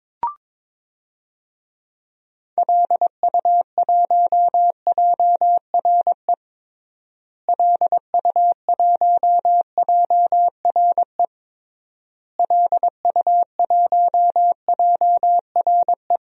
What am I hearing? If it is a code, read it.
Morse code, heard as a keyed tone, about 22 wpm, 700 Hz. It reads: LU1JRE LU1JRE LU1JRE
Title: DXCC-100 Call Signs - 6 Characters - Repeated 3x - 22wpm